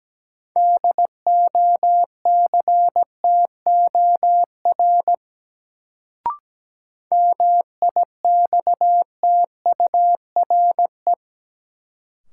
Morse code, heard as a keyed tone, about 17 wpm, 700 Hz. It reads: DOCTOR MIXTURE
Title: Sentences from Top 1500 Words - New Word Drill - 17wpm